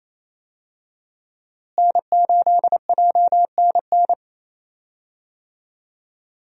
Morse code, heard as a keyed tone, about 28 wpm, 700 Hz. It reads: N8JNN
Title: USA Call Signs - 5 Characters - Repeated 2x - 28wpm